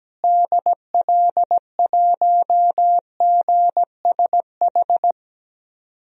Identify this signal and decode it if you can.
Morse code, heard as a keyed tone, about 17 wpm, 700 Hz. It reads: DL1GSH
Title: DXCC-100 Call Signs - 6 Characters - Repeated 2x - 17wpm